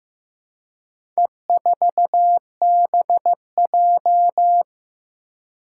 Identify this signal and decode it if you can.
Morse code, heard as a keyed tone, about 15 wpm, 700 Hz. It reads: E4BJ
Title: Four Challenging Letter-Number Combinations 15wpm